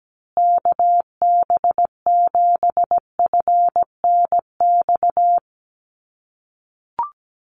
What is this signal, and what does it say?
Morse code, heard as a keyed tone, about 17 wpm, 700 Hz. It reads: KB7FNX
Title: USA Call Signs Repeated 2x 17wpm